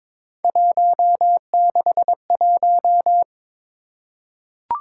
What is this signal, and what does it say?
Morse code, heard as a keyed tone, about 22 wpm, 700 Hz. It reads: 161
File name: practice / 22wpm / Three Letter-Number Combinations - 22wpm